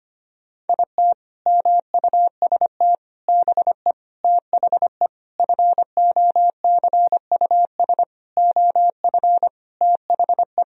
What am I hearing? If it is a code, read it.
Morse code, heard as a keyed tone, about 25 wpm, 700 Hz. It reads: IT MUST BE THE FOCUS OF THE
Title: Sentences from Top 1000 Words - Review Part 4 - 25wpm